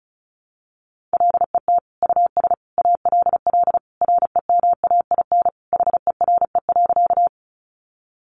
Morse code, heard as a keyed tone, about 34 wpm, 700 Hz. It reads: LET US ALL REMAIN HERE.
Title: Sentences from Top 1200 Words 35wpm